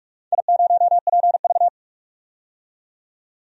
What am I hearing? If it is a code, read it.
Morse code, heard as a keyed tone, about 44 wpm, 700 Hz. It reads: I0PV
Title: DXCC-100 Call Signs - 4 Characters - 45wpm